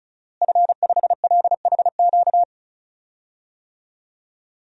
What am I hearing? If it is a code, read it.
Morse code, heard as a keyed tone, about 36 wpm, 700 Hz. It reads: F5LHQ
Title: DXCC-100 Call Signs - 5 Characters - Repeated 3x - 35wpm